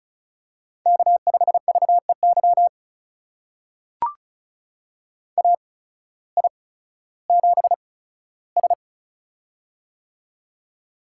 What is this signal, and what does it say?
Morse code, heard as a keyed tone, about 36 wpm, spaced as for 16 wpm, 700 Hz. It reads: K5VEY AI7S
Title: USA Call Signs - 8x Character Spacing - 35wpm